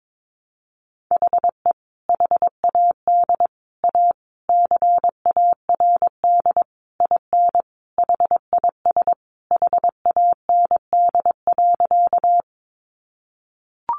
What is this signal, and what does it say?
Morse code, heard as a keyed tone, about 22 wpm, 700 Hz. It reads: HE HAD A CARD IN HIS HAND.
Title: Sentences from Top 600 Words 22wpm